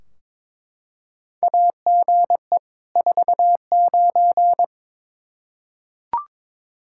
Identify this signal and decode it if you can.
Morse code, heard as a keyed tone, about 22 wpm, 700 Hz. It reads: AGE 49
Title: Practice QSO - Exchange 3 - More Info - 22wpm